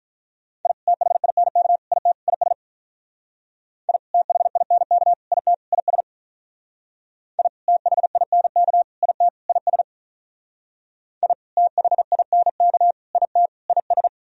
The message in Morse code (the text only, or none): I THINK IT IS I THINK IT IS I THINK IT IS I THINK IT IS